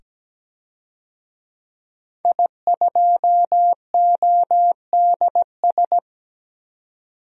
Morse code, heard as a keyed tone, about 17 wpm, 700 Hz. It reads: I2ODS
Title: DXCC-100 Call Signs - 5 Characters - 17wpm